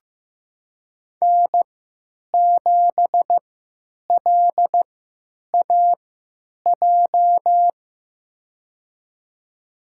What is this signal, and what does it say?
Morse code, heard as a keyed tone, about 15 wpm, spaced as for 9 wpm, 700 Hz. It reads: N7LAJ
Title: USA Call Signs - 5 Characters - 3x Character Spacing - 15wpm